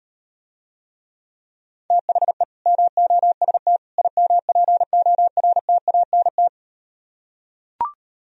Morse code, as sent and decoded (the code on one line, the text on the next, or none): - .... / -- --- ... - / .. -- .--. --- .-. - .- -. -
THE MOST IMPORTANT